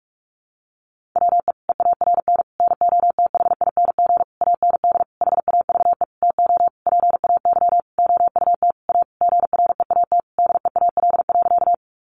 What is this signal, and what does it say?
Morse code, heard as a keyed tone, about 44 wpm, 700 Hz. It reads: WE EARN NOTHING AND HAVE TO PAY OUT A GREAT DEAL.
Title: Sentences from Top 1500 Words 45wpm